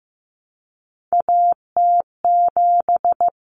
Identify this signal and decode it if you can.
Morse code, heard as a keyed tone, about 15 wpm, 700 Hz. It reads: AT7
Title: Common Cut Numbers - 3-Digit 15wpm